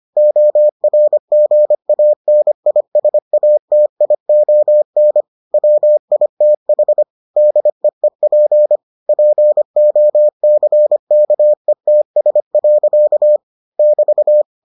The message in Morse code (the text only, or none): ORGANISATION WITH DEEP POCKETS. <BT>